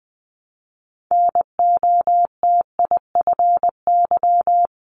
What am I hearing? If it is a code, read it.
Morse code, heard as a keyed tone, about 20 wpm, 700 Hz. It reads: NOTIFY
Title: Multi-letter Phonemes 20wpm